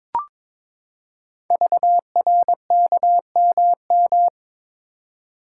Morse code, heard as a keyed tone, about 22 wpm, 700 Hz. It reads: VRKMM
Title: Code Groups - 5 Letters - 22wpm